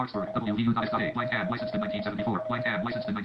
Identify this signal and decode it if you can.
Morse code, heard as a keyed tone, about 37 wpm, 630 Hz, with speech. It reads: DE WB8TJN